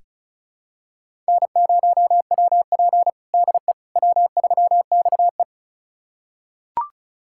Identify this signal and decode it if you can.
Morse code, heard as a keyed tone, about 36 wpm, 700 Hz. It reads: N0WP DE W3XE
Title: USA Callsign DE Callsign - 4 Characters - 35wpm